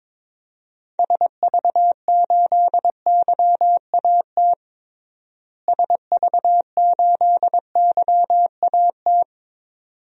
Morse code, heard as a keyed tone, about 22 wpm, 700 Hz. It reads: SV8YAT SV8YAT
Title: DXCC-100 Call Signs - 6 Characters - Repeated 2x - 22wpm